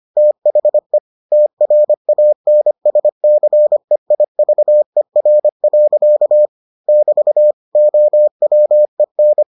Morse code, heard as a keyed tone, about 25 wpm, 600 Hz. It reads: THE TRANSCEIVER. <BT> OWEN